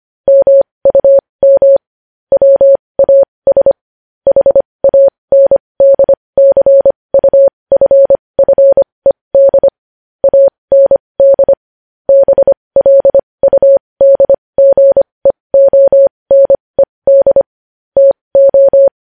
Morse code, heard as a keyed tone, about 25 wpm, spaced as for 20 wpm, 550 Hz. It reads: MUM WAS HANDCUFFED AND BLUDGEONED TO
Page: news headlines in morse code at 20 wpm